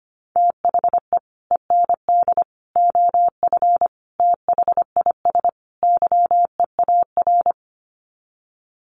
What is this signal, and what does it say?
Morse code, heard as a keyed tone, about 25 wpm, 700 Hz. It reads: THE END OF THIS YEAR
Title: Sets of 5 Words Encore 25wpm